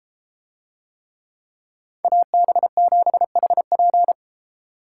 Morse code, heard as a keyed tone, about 32 wpm, 700 Hz. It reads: AB7HP